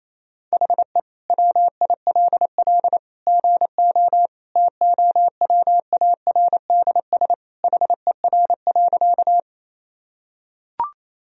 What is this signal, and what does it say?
Morse code, heard as a keyed tone, about 28 wpm, 700 Hz. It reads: HE WILL GO TOWARDS HER.
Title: Sentences from Top 1000 Words - Review Part 2 - 28wpm